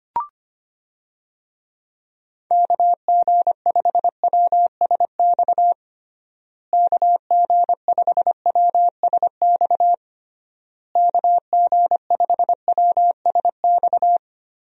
Morse code, heard as a keyed tone, about 25 wpm, 700 Hz. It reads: KG5WSX KG5WSX KG5WSX
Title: USA Call Signs - 6 Characters - Repeated 3x - 25wpm